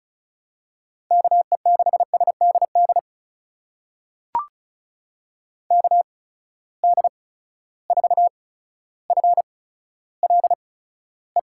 Morse code, heard as a keyed tone, about 36 wpm, spaced as for 15 wpm, 700 Hz. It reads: KE6SDD KD4FLE
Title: USA Call Signs - 6 Characters - 8x Character Spacing - 35wpm